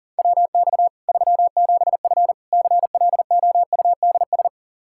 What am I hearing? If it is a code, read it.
Morse code, heard as a keyed tone, about 40 wpm, 700 Hz. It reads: WX 37F CLOUDS
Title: Practice QSO - Exchange 2 - Info - 40wpm